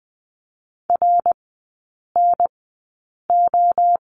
Morse code, heard as a keyed tone, about 20 wpm, spaced as for 8 wpm, 700 Hz. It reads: RNO
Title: 093 Lesson 16e - Letter c - Three Characters 20wpm